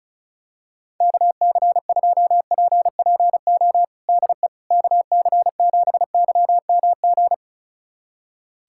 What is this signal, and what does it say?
Morse code, heard as a keyed tone, about 34 wpm, 700 Hz. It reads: KC2PPO DE KC7YMG
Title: USA Callsign DE Callsign - 6 Characters - 35wpm